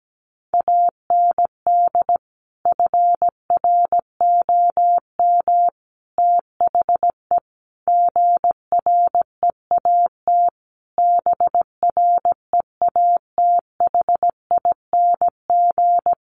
Morse code, heard as a keyed tone, about 17 wpm, 700 Hz. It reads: AND FROM THE GREAT BREATHING
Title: A Sound Of Thunder 17wpm